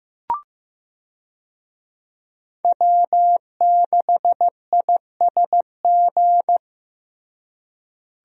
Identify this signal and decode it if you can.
Morse code, heard as a keyed tone, about 16 wpm, 700 Hz. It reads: W6ISG
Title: USA Call Signs - 5 Characters - Repeated 2x - 15wpm